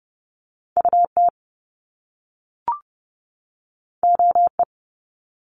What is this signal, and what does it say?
Morse code, heard as a keyed tone, about 30 wpm, 700 Hz. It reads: UT OE
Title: Top Two Letter Combinations in Words 30wpm